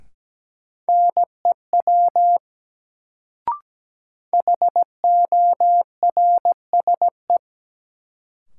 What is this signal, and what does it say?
Morse code, heard as a keyed tone, about 17 wpm, 700 Hz. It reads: NEW HORSE
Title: Sentences from Top 300 Words - New Word Drill - 17wpm